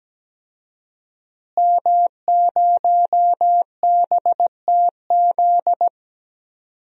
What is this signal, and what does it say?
Morse code, heard as a keyed tone, about 17 wpm, 700 Hz. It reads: M0BTZ
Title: DXCC-100 Call Signs - 5 Characters - Repeated 3x - 17wpm